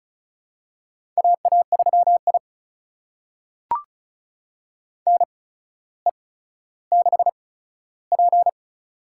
Morse code, heard as a keyed tone, about 36 wpm, spaced as for 15 wpm, 700 Hz. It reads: AA3I NE6P
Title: USA Call Signs - 4 Characters - 8x Character Spacing - 35wpm